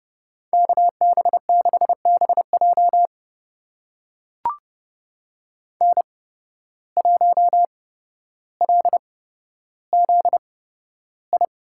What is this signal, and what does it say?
Morse code, heard as a keyed tone, about 30 wpm, spaced as for 13 wpm, 700 Hz. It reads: KB6BJ N1LZI
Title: USA Call Signs - 5 Characters - 8x Character Spacing - 30wpm